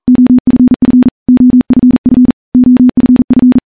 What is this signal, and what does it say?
Morse code, heard as a keyed tone, about 43 wpm, 250 Hz. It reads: OFF OFF OFF
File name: morse-training-40WPM